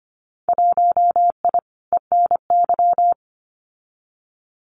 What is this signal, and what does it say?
Morse code, heard as a keyed tone, about 25 wpm, 700 Hz. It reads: 1I ENY
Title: Winter Field Day Exchanges Encore 25wpm